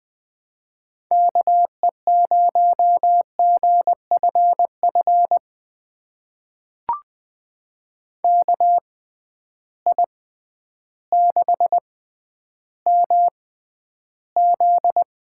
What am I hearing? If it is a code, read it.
Morse code, heard as a keyed tone, about 20 wpm, spaced as for 11 wpm, 700 Hz. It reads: KE0GFF KI6MZ
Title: USA Call Signs - 6 Characters - 6x Character Spacing - 20wpm